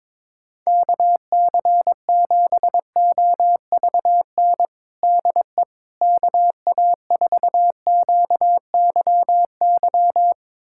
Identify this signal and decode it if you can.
Morse code, heard as a keyed tone, about 22 wpm, 700 Hz. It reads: KC7OVN DE KA4QYY
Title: USA Callsign DE Callsign 22wpm